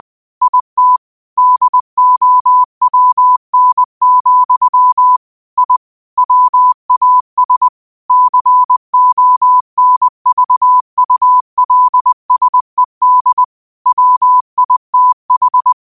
Morse code, heard as a keyed tone, about 20 wpm, 1000 Hz. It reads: IT DOWN, I WAS CONVULSED WITH
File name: qotdfiles / 20wpm